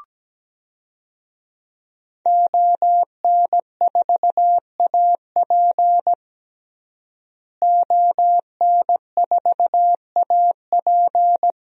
Morse code, heard as a keyed tone, about 17 wpm, 700 Hz. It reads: ON4AP ON4AP
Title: DXCC-100 Call Signs - 5 Characters - Repeated 2x - 17wpm